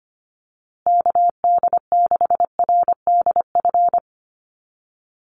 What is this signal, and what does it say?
Morse code, heard as a keyed tone, about 25 wpm, 700 Hz. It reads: KD6RDF
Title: USA Call Signs Repeated 3x 25wpm